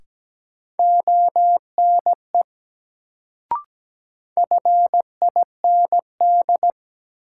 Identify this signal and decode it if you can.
Morse code, heard as a keyed tone, about 17 wpm, 700 Hz. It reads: ONE FIND
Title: Sentences from Top 100 Words - New Word Drill - 17wpm